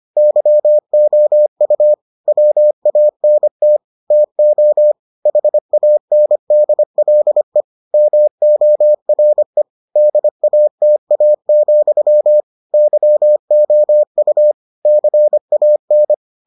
Morse code, heard as a keyed tone, about 25 wpm, 600 Hz. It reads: YOU WANT TO HANDLE MORE DATA, YOU CAN